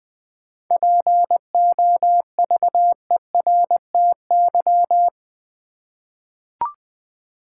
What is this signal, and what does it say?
Morse code, heard as a keyed tone, about 20 wpm, 700 Hz. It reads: POVERTY
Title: 7 Letter Words - First 3 Letters Given - 20wpm